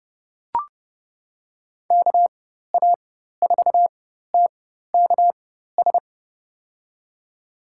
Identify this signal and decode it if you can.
Morse code, heard as a keyed tone, about 30 wpm, spaced as for 14 wpm, 700 Hz. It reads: KA4TKS
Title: USA Call Signs - 6 Characters - 4x Character Spacing - 30wpm